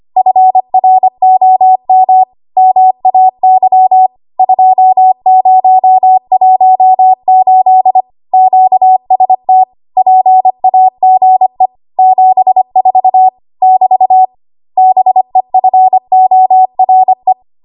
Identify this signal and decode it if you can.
Morse code, heard as a keyed tone, about 25 wpm, 750 Hz. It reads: FROM MAY 2018 QST PAGE 74 <BT> BEFORE
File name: Archive / 25 WPM / 25WPM